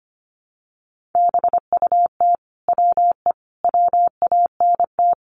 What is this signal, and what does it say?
Morse code, heard as a keyed tone, about 25 wpm, 700 Hz. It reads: BUT WE WANT
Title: Sentences from Top 700 Words 25wpm